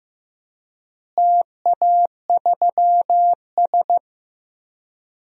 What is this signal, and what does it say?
Morse code, heard as a keyed tone, about 15 wpm, 700 Hz. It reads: TA3S
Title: DXCC-100 Call Signs - 4 Characters - Repeated 3x - 15wpm